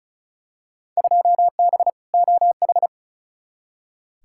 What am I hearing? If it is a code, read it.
Morse code, heard as a keyed tone, about 34 wpm, 700 Hz. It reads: 2B OH
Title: Field Day Exchanges Encore - 35wpm